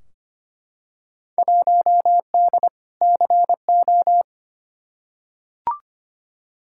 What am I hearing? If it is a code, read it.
Morse code, heard as a keyed tone, about 25 wpm, 700 Hz. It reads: 1D CO
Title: Field Day Exchanges - 25wpm